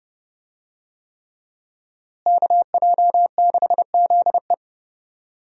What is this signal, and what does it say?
Morse code, heard as a keyed tone, about 31 wpm, 700 Hz. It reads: KJ6ZE
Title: USA Call Signs - 5 Characters - Repeated 3x - 30wpm